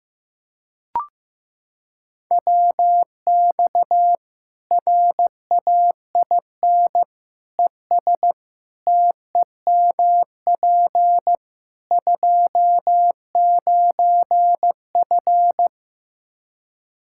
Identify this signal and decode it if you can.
Morse code, heard as a keyed tone, about 15 wpm, 700 Hz. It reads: WX RAIN ES TEMP 29F
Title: Practice QSO - Exchange 2 - Info - 15wpm